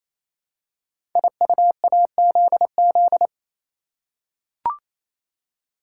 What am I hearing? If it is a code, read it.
Morse code, heard as a keyed tone, about 27 wpm, 700 Hz. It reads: IUAZZ
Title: Code Groups - 5 Letters - 28wpm